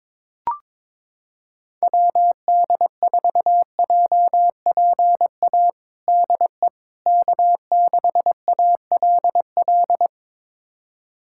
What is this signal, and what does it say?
Morse code, heard as a keyed tone, about 22 wpm, 700 Hz. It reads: WD4JPA DE K6ALL
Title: USA Callsign DE Callsign 22wpm